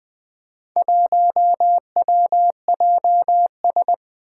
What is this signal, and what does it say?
Morse code, heard as a keyed tone, about 21 wpm, 700 Hz. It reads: 1WJS
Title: Four Challenging Letter-Number Combinations 20wpm